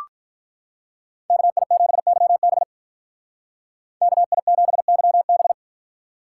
Morse code, heard as a keyed tone, about 43 wpm, 700 Hz. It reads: KI7YB KI7YB
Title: Speed Racing - USA Call Signs 35wpm